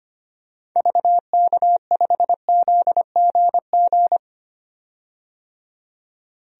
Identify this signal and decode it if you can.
Morse code, heard as a keyed tone, about 25 wpm, 700 Hz. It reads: VK5ZGG
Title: DXCC-100 Call Signs - 6 Characters - 25wpm